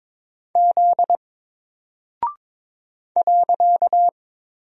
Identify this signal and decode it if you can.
Morse code, heard as a keyed tone, about 22 wpm, 700 Hz. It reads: Z .
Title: Single Character (Letter:Number:Punctuation:Prosign) 22wpm